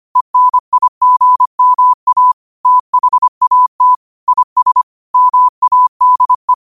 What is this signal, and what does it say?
Morse code, heard as a keyed tone, about 25 wpm, 1000 Hz. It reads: ENIGMA THAT IS MADE
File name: 25wpm